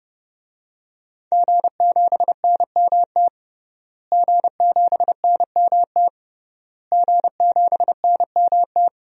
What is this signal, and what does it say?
Morse code, heard as a keyed tone, about 30 wpm, 700 Hz. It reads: G7NMT G7NMT G7NMT
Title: DXCC-100 Call Signs - 5 Characters - Repeated 3x - 30wpm